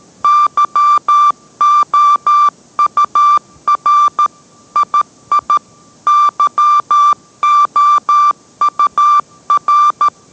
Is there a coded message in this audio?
Yes: Morse code from 0.2 to 10.1 s, heard as a keyed tone, about 13 wpm, 1180 Hz. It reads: YOUR II YOUR